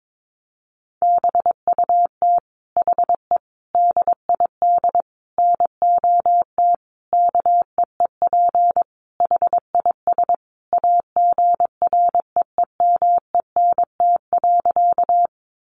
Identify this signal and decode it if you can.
Morse code, heard as a keyed tone, about 22 wpm, 700 Hz. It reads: BUT HE DID NOT KEEP HIS AGREEMENT.
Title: Sentences from Top 1500 Words 22wpm